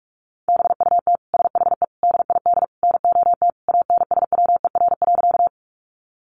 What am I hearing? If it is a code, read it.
Morse code, heard as a keyed tone, about 46 wpm, 700 Hz. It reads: BUT SHE DID NOT ANSWER.
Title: Sentences from Top 600 Words 45wpm